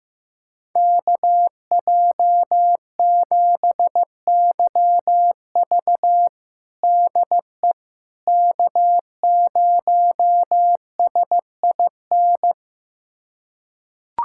Morse code, heard as a keyed tone, about 15 wpm, 700 Hz. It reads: KJ7YV DE K0SIN
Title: USA Callsign DE Callsign - 5 Characters - 15wpm